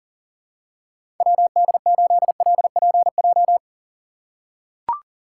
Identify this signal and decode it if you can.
Morse code, heard as a keyed tone, about 40 wpm, 700 Hz. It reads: WD8LPJ
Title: USA Call Signs - 6 Characters - 4x Character Spacing - 40wpm